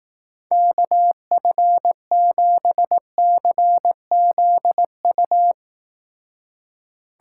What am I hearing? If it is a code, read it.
Morse code, heard as a keyed tone, about 18 wpm, 700 Hz. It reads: KF7CZU
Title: Speed Racing - USA 2x3 Call Signs 15wpm